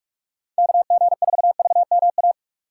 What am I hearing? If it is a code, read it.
Morse code, heard as a keyed tone, about 46 wpm, 700 Hz. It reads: KG4VMA
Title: USA Call Signs - 6 Characters - 2x Character Spacing - 45wpm